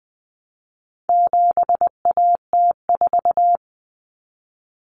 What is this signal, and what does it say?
Morse code, heard as a keyed tone, about 19 wpm, 700 Hz. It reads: 7AT4
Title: Common Cut Numbers - 4-Digit 20wpm